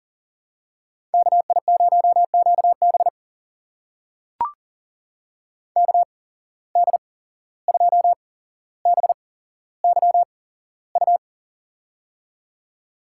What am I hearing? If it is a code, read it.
Morse code, heard as a keyed tone, about 38 wpm, spaced as for 16 wpm, 700 Hz. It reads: KI0QB KD2BYU
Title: USA Call Signs - 8x Character Spacing - 40wpm